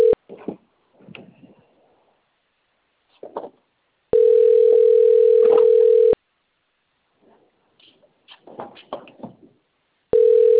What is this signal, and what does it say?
Ringback tone (North America), 440 + 480 Hz.